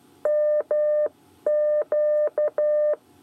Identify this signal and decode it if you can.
Morse code, heard as a keyed tone, about 12 wpm, 600 Hz. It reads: MQ
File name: 8 bis lezione cw 1